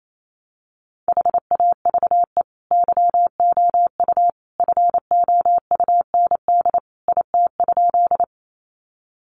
Morse code, heard as a keyed tone, about 28 wpm, 700 Hz. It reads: HAVE YOU FOUND IT?
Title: Sentences from Top 300 Words 28wpm